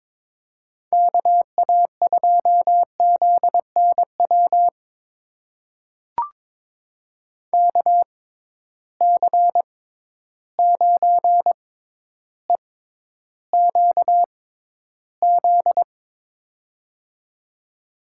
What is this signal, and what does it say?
Morse code, heard as a keyed tone, about 21 wpm, spaced as for 11 wpm, 700 Hz. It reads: KA2ZNW KC9EQZ
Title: USA Call Signs - 6 Characters - 6x Character Spacing - 22wpm